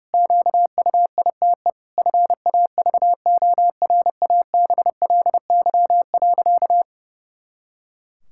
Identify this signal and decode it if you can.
Morse code, heard as a keyed tone, about 30 wpm, 700 Hz. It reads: QUITE FAVORABLY.
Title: Pangram Challenge v2 - 30wpm